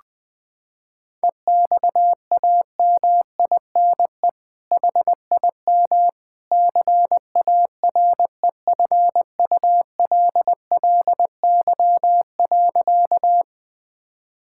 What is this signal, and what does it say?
Morse code, heard as a keyed tone, about 20 wpm, 700 Hz. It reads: EXAMINE HIM CAREFULLY.
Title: Sentences from Top 1600 Words 20wpm